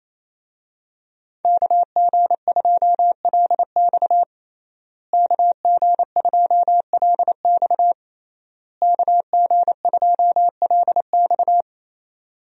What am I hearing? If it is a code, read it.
Morse code, heard as a keyed tone, about 27 wpm, 700 Hz. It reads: KG2LX KG2LX KG2LX
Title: USA Call Signs - 5 Characters - Repeated 3x - 28wpm